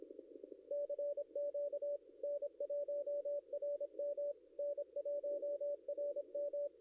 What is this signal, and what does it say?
Morse code, heard as a keyed tone, about 25 wpm, 560 Hz. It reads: CQ N1RM N1RM
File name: contests / ss-cw-k5tr-op-n5rz -0700